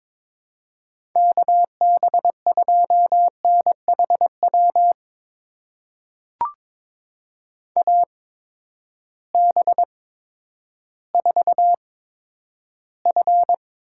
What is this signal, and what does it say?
Morse code, heard as a keyed tone, about 22 wpm, spaced as for 11 wpm, 700 Hz. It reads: KB2NHW AB4F
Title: USA Call Signs - 8x Character Spacing - 22wpm